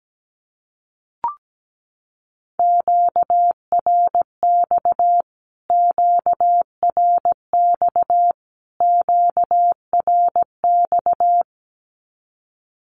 Morse code, heard as a keyed tone, about 17 wpm, 700 Hz. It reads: QRX QRX QRX
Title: Q-Codes Repeated 3x 17wpm